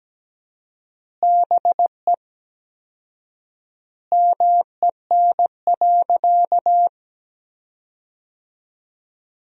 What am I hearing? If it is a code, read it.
Morse code, heard as a keyed tone, about 17 wpm, 700 Hz. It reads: BE MEN.
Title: 4x Word-Spacing - Sentences from Top 300 Words - 17wpm